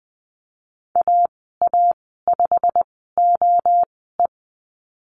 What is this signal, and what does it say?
Morse code, heard as a keyed tone, about 20 wpm, spaced as for 15 wpm, 700 Hz. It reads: AA5OE
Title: 122 Lesson 20g - Number 3 - Call Signs 20wpm